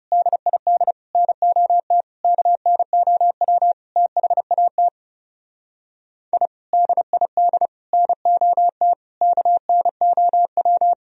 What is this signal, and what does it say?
Morse code, heard as a keyed tone, about 40 wpm, 700 Hz. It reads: DID NOT KNOW THAT I DID NOT KNOW